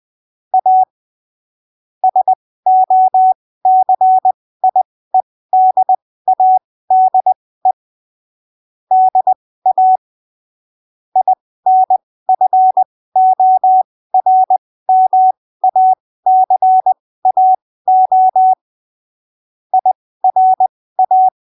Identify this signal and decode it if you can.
Morse code, heard as a keyed tone, about 20 wpm, spaced as for 15 wpm, 750 Hz. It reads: A SOCIEDADE DA INFORMACAO IRA
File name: arquivos / texto / texto 30 20ppm